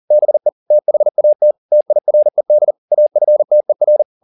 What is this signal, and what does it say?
Morse code, heard as a keyed tone, about 40 wpm, 600 Hz. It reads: BE THAT TIRED AFTER